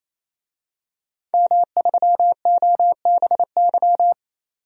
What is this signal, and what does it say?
Morse code, heard as a keyed tone, about 28 wpm, 700 Hz. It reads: M3OBY
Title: DXCC-100 Call Signs - 5 Characters - 28wpm